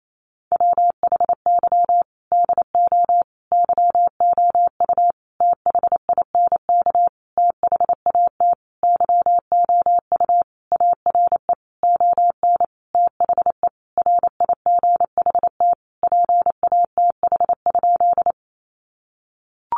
Morse code, heard as a keyed tone, about 28 wpm, 700 Hz. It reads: WHY DO YOU THINK THAT YOU ARE ON THE RIGHT PATH?
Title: Sentences from Top 1400 Words 28wpm